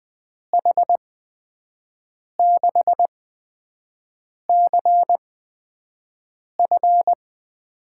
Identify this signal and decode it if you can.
Morse code, heard as a keyed tone, about 20 wpm, spaced as for 5 wpm, 700 Hz. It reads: H6CF
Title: USA Call Signs - 8x Character Spacing - 20wpm